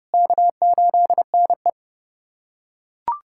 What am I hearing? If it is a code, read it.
Morse code, heard as a keyed tone, about 30 wpm, 700 Hz. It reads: K8NE
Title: USA Callsign DE Callsign - 4 Characters - 30wpm